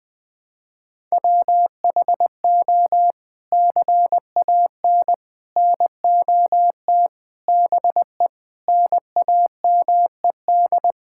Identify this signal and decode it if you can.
Morse code, heard as a keyed tone, about 20 wpm, 700 Hz. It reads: WHO CAN NOT BE NAMED
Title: Sets of 8 Words - 20wpm